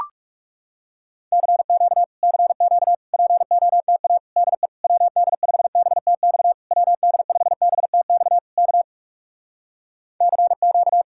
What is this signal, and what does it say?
Morse code, heard as a keyed tone, about 40 wpm, 700 Hz. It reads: CQ CQ POTA DE WD5BTX WD5BTX K CQ